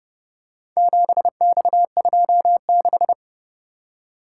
Morse code, heard as a keyed tone, about 30 wpm, 700 Hz. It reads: 7X26